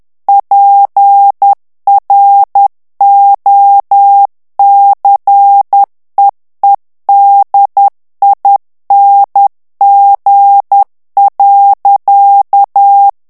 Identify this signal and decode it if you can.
Morse code, heard as a keyed tone, about 10 wpm, 800 Hz. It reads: PROCEEDING.
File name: ANGLAIS texte clair 3 vitesse 10 mots minute